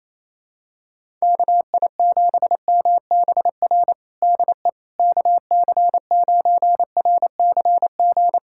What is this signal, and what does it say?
Morse code, heard as a keyed tone, about 28 wpm, 700 Hz. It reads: KI7MBR DE KC9RCG
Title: USA Callsign DE Callsign - 6 Characters - 28wpm